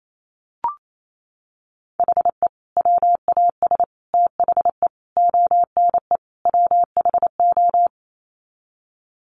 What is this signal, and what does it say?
Morse code, heard as a keyed tone, about 28 wpm, 700 Hz. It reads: HE WAS THE ONE WHO
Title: Sets of 5 Words Encore2 28wpm